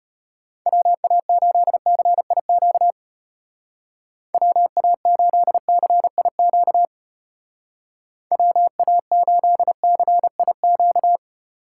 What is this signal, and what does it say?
Morse code, heard as a keyed tone, about 33 wpm, 700 Hz. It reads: WA8CIQ WA8CIQ WA8CIQ